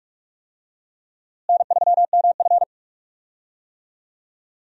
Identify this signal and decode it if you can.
Morse code, heard as a keyed tone, about 41 wpm, 700 Hz. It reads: N3MF
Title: USA Call Signs - 4 Characters - Repeated 3x - 45wpm